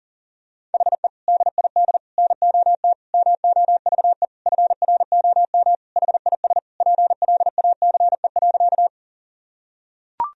Morse code, heard as a keyed tone, about 37 wpm, 700 Hz. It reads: HE DID NOT MOVE FROM HIS PLACE.
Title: Sentences from Top 1000 Words - Review Part 1 - 40wpm